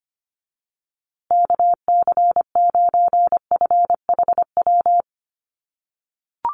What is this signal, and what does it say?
Morse code, heard as a keyed tone, about 25 wpm, 700 Hz. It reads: KC9FHW
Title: USA Call Signs Repeated 3x 25wpm